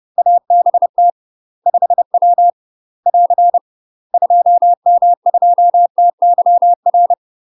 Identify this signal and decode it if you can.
Morse code, heard as a keyed tone, about 30 wpm, 700 Hz. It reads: ABT 5W <AR> 2M2TYR
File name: UK MASTER QSOs TRAINING / 30WPM / UK MASTER QSOs 0002